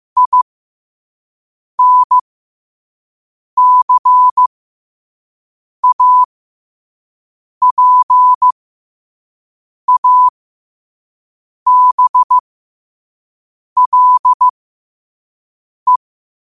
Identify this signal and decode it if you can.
Morse code, heard as a keyed tone, about 15 wpm, spaced as for 5 wpm, 1000 Hz. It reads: INCAPABLE
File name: qotdfiles / 5wpm